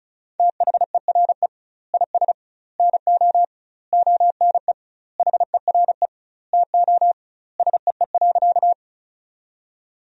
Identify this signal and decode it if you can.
Morse code, heard as a keyed tone, about 34 wpm, 700 Hz. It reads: THERE IS NO ONE HERE TO SEE.
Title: 2x Word-Spacing - Sentences from Top 200 Words - 35wpm